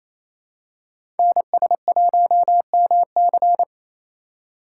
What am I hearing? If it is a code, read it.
Morse code, heard as a keyed tone, about 28 wpm, 700 Hz. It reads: NS1MC
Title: USA Call Signs - 5 Characters - 4x Character Spacing - 28wpm